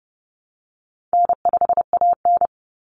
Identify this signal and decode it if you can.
Morse code, heard as a keyed tone, about 29 wpm, 700 Hz. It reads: N5AN
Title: Easy Call Signs Repeated 3x - 30wpm